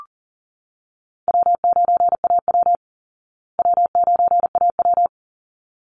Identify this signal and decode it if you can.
Morse code, heard as a keyed tone, about 38 wpm, 700 Hz. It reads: W9AW W9AW
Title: Call Signs Repeated 3x - 40wpm